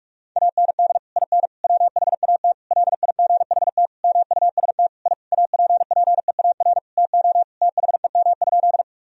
Morse code, heard as a keyed tone, about 41 wpm, 700 Hz. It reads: AND IN WHAT LIGHT MUST I APPEAR TO THEM?
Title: Sentences from Top 1000 Words - Review Part 2 - 45wpm